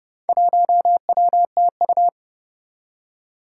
Morse code, heard as a keyed tone, about 30 wpm, 700 Hz. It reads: 1WTU
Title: Four Challenging Letter-Number Combinations 30wpm